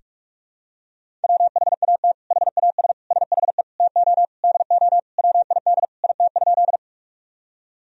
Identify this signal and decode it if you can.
Morse code, heard as a keyed tone, about 46 wpm, 700 Hz. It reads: WHAT HAS SHE TO DO WID IT?
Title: Sentences from Top 100 Words and Abbreviations 45wpm